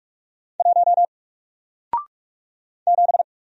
Morse code, heard as a keyed tone, about 44 wpm, 700 Hz. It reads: 1 7
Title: Single Letter-Number Alt 45wpm